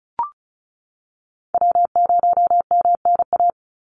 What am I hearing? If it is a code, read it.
Morse code, heard as a keyed tone, about 34 wpm, 700 Hz. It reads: W0MNA
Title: NA SOTA Chaser Callsigns 35wpm